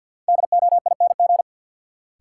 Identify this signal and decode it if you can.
Morse code, heard as a keyed tone, about 51 wpm, 700 Hz. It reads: DOING
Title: Sentences from Top 1400 Words - New Word Drill - 50wpm